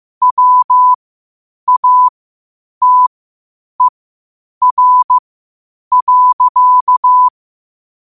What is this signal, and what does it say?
Morse code, heard as a keyed tone, about 15 wpm, spaced as for 8 wpm, 1000 Hz. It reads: WATER.
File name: qotdfiles / 8wpm